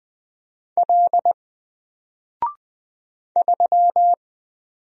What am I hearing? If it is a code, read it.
Morse code, heard as a keyed tone, about 20 wpm, 700 Hz. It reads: L 3
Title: Single Character (Letter:Number:Punctuation:Prosign) Alt 20wpm